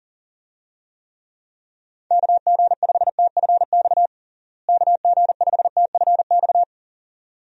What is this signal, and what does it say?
Morse code, heard as a keyed tone, about 37 wpm, 700 Hz. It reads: KG5TFX KG5TFX
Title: USA Call Signs - 6 Characters - Repeated 2x - 40wpm